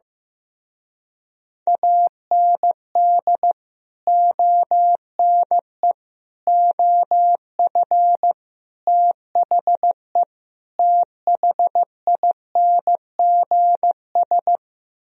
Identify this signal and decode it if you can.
Morse code, heard as a keyed tone, about 15 wpm, 700 Hz. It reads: AND ONE OF THE THINGS
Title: Speed-Racing - Sets of 5 Words 15wpm